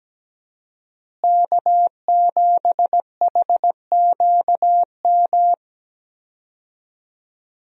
Morse code, heard as a keyed tone, about 17 wpm, 700 Hz. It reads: K7HQM
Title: USA Call Signs - 5 Characters - Repeated 2x - 17wpm